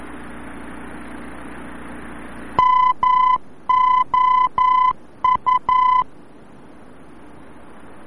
Morse code, heard as a keyed tone, about 11 wpm, 1010 Hz. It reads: MOU
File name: france / mx / MOU-VOR-F